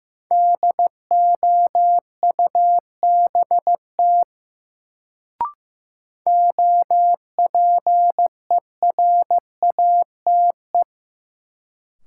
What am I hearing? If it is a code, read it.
Morse code, heard as a keyed tone, about 16 wpm, 700 Hz. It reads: DOUBT OPERATE
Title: Sentences from Top 1500 Words - New Word Drill Alt - 15wpm